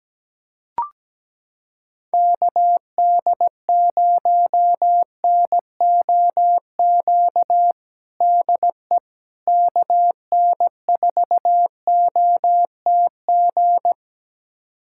Morse code, heard as a keyed tone, about 17 wpm, 700 Hz. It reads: KD0NOQ DE KN4OTG
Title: USA Callsign DE Callsign - 6 Characters - 17wpm